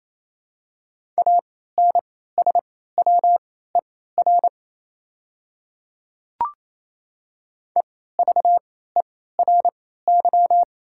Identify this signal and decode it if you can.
Morse code, heard as a keyed tone, about 28 wpm, spaced as for 16 wpm, 690 Hz. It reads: ANSWER EVERY
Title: 3x Character Spacing - Top 300 Words - 28wpm